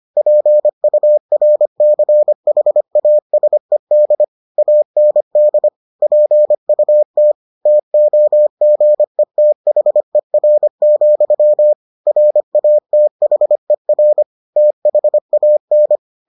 Morse code, heard as a keyed tone, about 25 wpm, 600 Hz. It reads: PURCHASED AND PUT TOGETHER, RATHER THAN